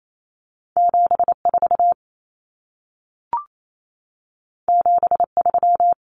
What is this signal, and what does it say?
Morse code, heard as a keyed tone, about 28 wpm, 700 Hz. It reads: 74 73
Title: Common Cut Numbers - 2-Digit 28wpm